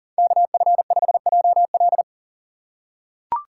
Morse code, heard as a keyed tone, about 40 wpm, 700 Hz. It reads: KF5JL
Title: USA Call Signs - 5 Characters - 6x Character Spacing - 40wpm